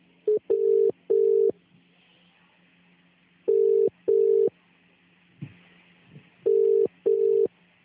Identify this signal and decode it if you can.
Telephone call progress tone (UK), heard as a telephone network's tone, about 400, 450 Hz. Ringback tone.